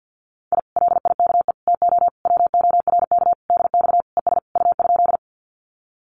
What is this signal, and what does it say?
Morse code, heard as a keyed tone, about 50 wpm, 700 Hz. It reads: I LIKE TO WORK DX ES U?
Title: Common QSO Phrases Encore 50wpm